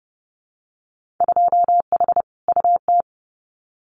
Morse code, heard as a keyed tone, about 32 wpm, 700 Hz. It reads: 2H UT
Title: Winter Field Day Exchanges Encore 30wpm